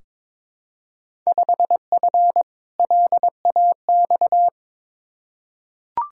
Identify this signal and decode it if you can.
Morse code, heard as a keyed tone, about 22 wpm, 700 Hz. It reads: 5F LAX
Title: Field Day Exchanges Encore - 22wpm